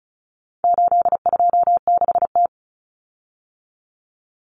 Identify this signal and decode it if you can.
Morse code, heard as a keyed tone, about 34 wpm, 700 Hz. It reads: 826T
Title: Common Cut Numbers - 4-Digit 35wpm